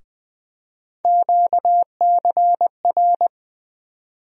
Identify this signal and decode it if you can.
Morse code, heard as a keyed tone, about 20 wpm, 700 Hz. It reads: QCR
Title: USA Call Sign Postfixes - 3 Letters 20wpm